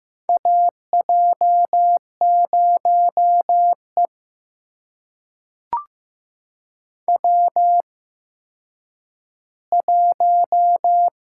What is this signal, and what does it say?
Morse code, heard as a keyed tone, about 15 wpm, spaced as for 9 wpm, 700 Hz. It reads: AJ0E W1
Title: USA Call Signs - 4 Characters - 8x Character Spacing - 15wpm